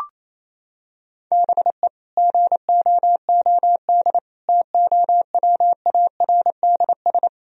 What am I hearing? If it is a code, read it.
Morse code, heard as a keyed tone, about 28 wpm, 700 Hz. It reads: BE GOOD TOWARDS
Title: Sentences from Top 1000 Words - Review Part 2 - 28wpm